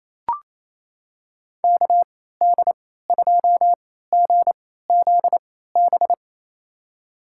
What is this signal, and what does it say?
Morse code, heard as a keyed tone, about 28 wpm, spaced as for 16 wpm, 700 Hz. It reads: KD2GZB
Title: USA Call Signs - 6 Characters - 3x Character Spacing - 28wpm